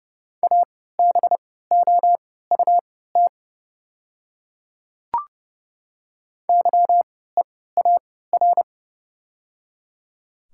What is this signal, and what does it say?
Morse code, heard as a keyed tone, about 30 wpm, spaced as for 17 wpm, 690 Hz. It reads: ABOUT YEAR
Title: 3x Character Spacing - Top 200 Words - 30wpm